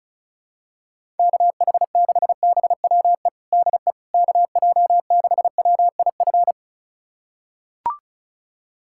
Morse code, heard as a keyed tone, about 36 wpm, 700 Hz. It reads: KH6BWE DE KJ6WIF
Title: USA Callsign DE Callsign 35wpm